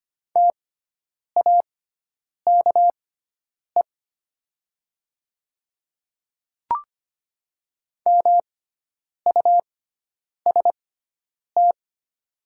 Morse code, heard as a keyed tone, about 25 wpm, spaced as for 9 wpm, 690 Hz. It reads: TAKE MUST
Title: 6x Character Spacing - Top 200 Words - 25wpm